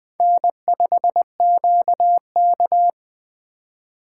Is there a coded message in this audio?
Yes: Morse code from 0.2 to 2.9 s, heard as a keyed tone, about 19 wpm, 700 Hz. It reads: N5QK